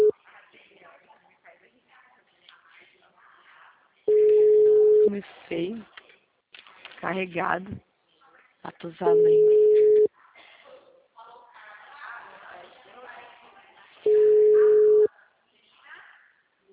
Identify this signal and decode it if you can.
Ringback tone (Europe, 425 Hz), with speech.
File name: out- -2050- - - .18729